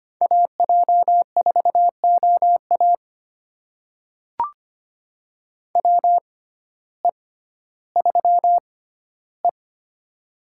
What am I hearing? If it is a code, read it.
Morse code, heard as a keyed tone, about 25 wpm, spaced as for 14 wpm, 700 Hz. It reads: AJ4OA WE3E
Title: USA Call Signs - 6x Character Spacing - 25wpm